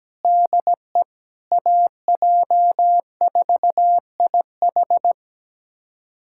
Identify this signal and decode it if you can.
Morse code, heard as a keyed tone, about 17 wpm, 700 Hz. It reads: DE AJ4IH
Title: USA Callsign DE Callsign - 5 Characters - 17wpm